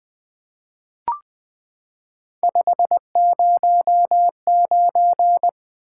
Morse code, heard as a keyed tone, about 20 wpm, 700 Hz. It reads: 509
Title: Three Letter-Number Combinations - 20wpm